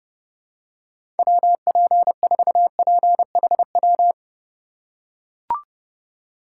Morse code, heard as a keyed tone, about 32 wpm, 700 Hz. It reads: WP4PHW
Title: USA Call Signs - 6 Characters - 6x Character Spacing - 30wpm